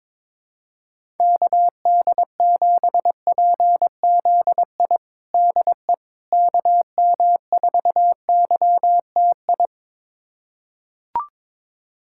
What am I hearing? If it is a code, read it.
Morse code, heard as a keyed tone, about 22 wpm, 700 Hz. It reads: KD7PZI DE KM4YTI
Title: USA Callsign DE Callsign - 6 Characters - 22wpm